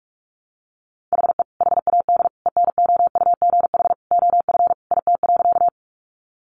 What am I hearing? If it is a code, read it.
Morse code, heard as a keyed tone, about 44 wpm, 700 Hz. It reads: HE HAD ENOUGH OF IT.
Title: Sentences from Top 400 Words 45wpm